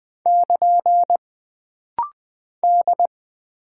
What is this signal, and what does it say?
Morse code, heard as a keyed tone, about 20 wpm, 700 Hz. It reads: <KN> D
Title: Single Character (Letter:Number:Punctuation:Prosign) - Rapid-Fire - 20wpm